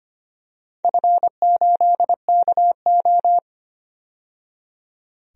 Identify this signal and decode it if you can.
Morse code, heard as a keyed tone, about 25 wpm, 700 Hz. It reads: F8KO
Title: DXCC-100 Call Signs - 4 Characters - 25wpm